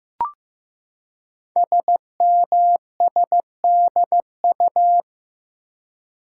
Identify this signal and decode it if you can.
Morse code, heard as a keyed tone, about 15 wpm, 700 Hz. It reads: SMSDU
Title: Code Groups - 5 Letters - 15wpm